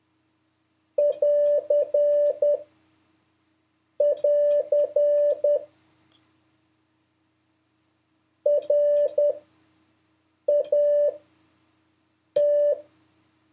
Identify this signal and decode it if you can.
Morse code, heard as a keyed tone, about 10 wpm, spaced as for 5 wpm, 580 Hz. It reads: <AR><AR> RAT
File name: 5of5-test